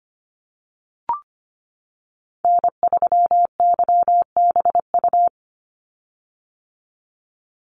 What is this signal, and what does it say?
Morse code, heard as a keyed tone, about 25 wpm, 700 Hz. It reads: N3YBU
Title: USA Call Signs 25wpm